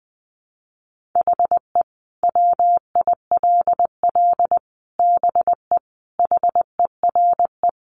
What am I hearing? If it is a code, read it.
Morse code, heard as a keyed tone, about 20 wpm, 700 Hz. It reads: HE WILL BE HERE
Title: Sentences from Top 600 Words 20wpm